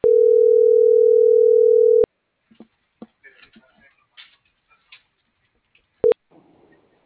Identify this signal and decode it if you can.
Ringback tone (North America), 440 + 480 Hz.